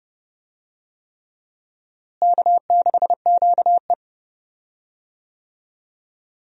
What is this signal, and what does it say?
Morse code, heard as a keyed tone, about 30 wpm, 700 Hz. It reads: K6QE
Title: USA Call Signs - 4 Characters - Repeated 3x - 30wpm